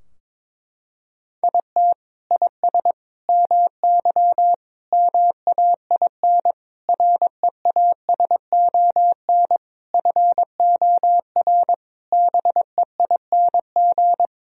Morse code, heard as a keyed tone, about 22 wpm, 700 Hz. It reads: IT IS MY MAIN REASON FOR BEING
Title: Sentences from Top 1000 Words - Review Part 2 - 22wpm